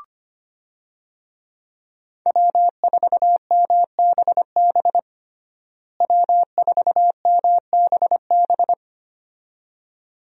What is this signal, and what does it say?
Morse code, heard as a keyed tone, about 25 wpm, 700 Hz. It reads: W4MBB W4MBB
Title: USA Call Signs - 5 Characters - Repeated 2x - 25wpm